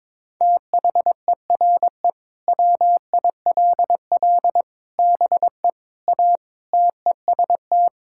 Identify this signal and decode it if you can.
Morse code, heard as a keyed tone, about 22 wpm, 700 Hz. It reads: THERE WILL BE A TEST
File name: practice / 22wpm / Sentences with 3-Word Phrases - 22wpm